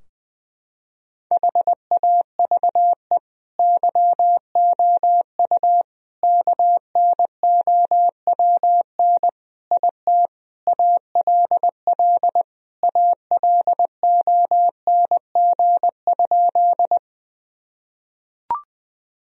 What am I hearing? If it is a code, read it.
Morse code, heard as a keyed tone, about 20 wpm, 700 Hz. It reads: HAVE YOU KNOWN IT ALL ALONG?
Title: Sentences from Top 1000 Words - Review Part 3 - 20wpm